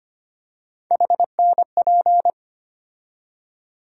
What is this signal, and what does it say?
Morse code, heard as a keyed tone, about 25 wpm, 700 Hz. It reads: HNP
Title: USA Call Sign Postfixes - 3 Letters 25wpm